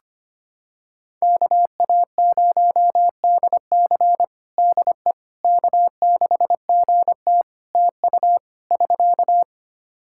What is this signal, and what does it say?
Morse code, heard as a keyed tone, about 25 wpm, 700 Hz. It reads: KA0DC DE K6GT TU <SK>
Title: Practice QSO - Exchange 4 - Ending - 25wpm